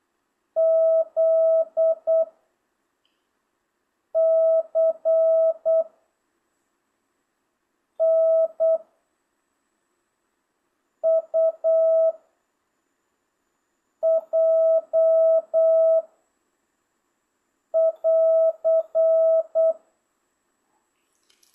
Morse code, heard as a keyed tone, about 8 wpm, 650 Hz. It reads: Z C N U J <AR>